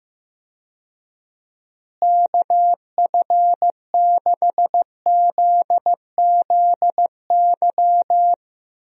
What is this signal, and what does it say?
Morse code, heard as a keyed tone, about 15 wpm, 700 Hz. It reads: KF6ZZY
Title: USA Call Signs - 6 Characters - Repeated 3x - 15wpm